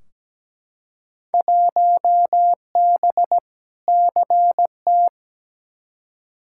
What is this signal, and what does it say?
Morse code, heard as a keyed tone, about 17 wpm, 700 Hz. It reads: 1B CT
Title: Field Day Exchanges Encore - 17wpm